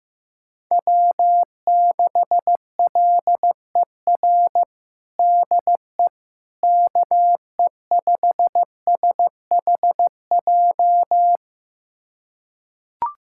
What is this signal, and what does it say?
Morse code, heard as a keyed tone, about 15 wpm, 700 Hz. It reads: W6LER DE KE5SHJ
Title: USA Callsign DE Callsign 15wpm